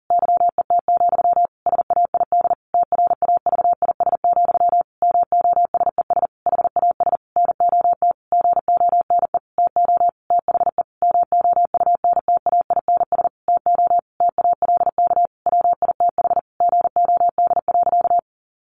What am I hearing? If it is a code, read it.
Morse code, heard as a keyed tone, about 37 wpm, 700 Hz. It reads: YET, SAID TRAVIS, MOSES HAS NOT GONE TO THE MOUNTAINS TO TALK WITH GOD.
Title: A Sound Of Thunder 40wpm